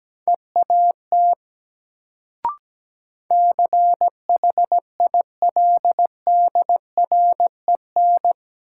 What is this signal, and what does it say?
Morse code, heard as a keyed tone, about 17 wpm, 700 Hz. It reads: EAT CHILDREN
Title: Sentences from Top 300 Words - New Word Drill - 17wpm